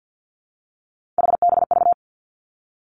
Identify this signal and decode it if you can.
Morse code, heard as a keyed tone, about 50 wpm, 700 Hz. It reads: HBV
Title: Three Challenging Letter-Number Combinations 50wpm